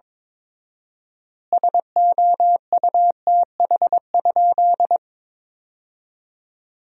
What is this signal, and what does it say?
Morse code, heard as a keyed tone, about 22 wpm, 700 Hz. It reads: SOUTH?
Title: 4x Word-Spacing - Sentences from Top 400 Words - 22wpm